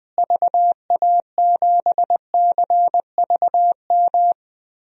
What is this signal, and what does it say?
Morse code, heard as a keyed tone, about 20 wpm, 700 Hz. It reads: VA7CVM
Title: DXCC-100 Call Signs - 6 Characters - Repeated 2x - 20wpm